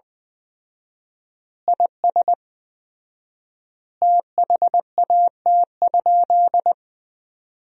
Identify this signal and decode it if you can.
Morse code, heard as a keyed tone, about 20 wpm, 700 Hz. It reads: IS THAT?
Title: 4x Word-Spacing - Sentences from Top 100 Words - 20wpm